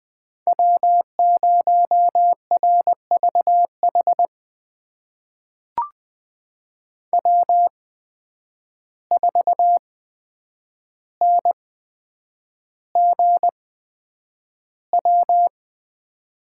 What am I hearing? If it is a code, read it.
Morse code, heard as a keyed tone, about 20 wpm, spaced as for 9 wpm, 700 Hz. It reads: W0RVH W4NGW
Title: USA Call Signs - 5 Characters - 8x Character Spacing - 20wpm